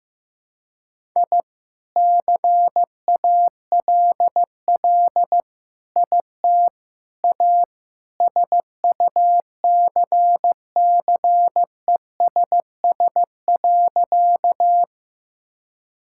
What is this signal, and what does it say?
Morse code, heard as a keyed tone, about 15 wpm, 700 Hz. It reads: I CALL IT A SUCCESS.
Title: Sentences from Top 1000 Words - Review Part 6 - 15wpm